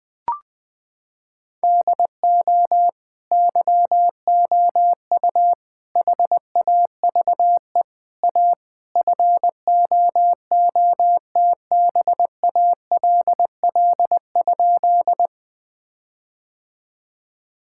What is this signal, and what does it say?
Morse code, heard as a keyed tone, about 20 wpm, 700 Hz. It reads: DO YOU HAVE A FOOTBALL?
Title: Sentences from Top 1600 Words 20wpm